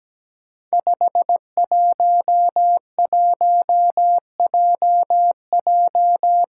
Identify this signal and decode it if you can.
Morse code, heard as a keyed tone, about 17 wpm, 700 Hz. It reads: 511JJ
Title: Five Challenging Letter-Number Combinations 17wpm